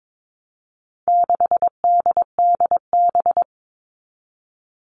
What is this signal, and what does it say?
Morse code, heard as a keyed tone, about 22 wpm, 700 Hz. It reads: 6DDB
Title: All Cut Numbers - 4-Digit 22wpm